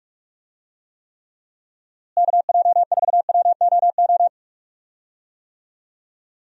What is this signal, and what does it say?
Morse code, heard as a keyed tone, about 43 wpm, 700 Hz. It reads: KJ4WOO
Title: USA Call Signs - 6 Characters - Repeated 3x - 45wpm